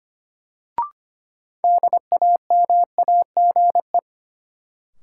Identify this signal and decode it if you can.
Morse code, heard as a keyed tone, about 25 wpm, 700 Hz. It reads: DAMAGE
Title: Sentences from Top 1100 Words - New Word Drill - 25wpm